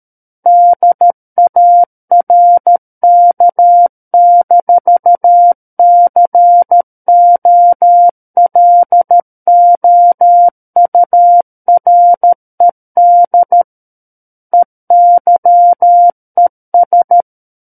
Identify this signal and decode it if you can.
Morse code, heard as a keyed tone, about 13 wpm, 700 Hz. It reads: DARK-COLOURED EYES
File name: Book One / 04-Chapter 4 - 13 WPM